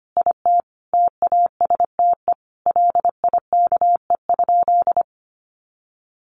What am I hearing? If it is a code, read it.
Morse code, heard as a keyed tone, about 25 wpm, 690 Hz. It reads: IT TASTE LIKE?
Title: Sentences from Top 1100 Words 25wpm